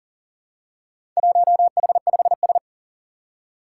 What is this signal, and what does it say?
Morse code, heard as a keyed tone, about 40 wpm, 700 Hz. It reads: 1H5S
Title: Four Challenging Letter-Number Combinations 40wpm